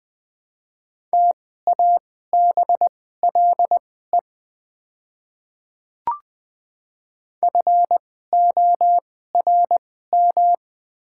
Morse code, heard as a keyed tone, about 20 wpm, spaced as for 14 wpm, 690 Hz. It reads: TABLE FORM
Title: 2x Character Spacing - Top 500 Words - 20wpm